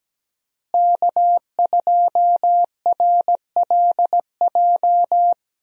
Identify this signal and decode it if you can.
Morse code, heard as a keyed tone, about 17 wpm, 700 Hz. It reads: K2RLJ
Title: USA Call Signs - 5 Characters - Repeated 2x - 17wpm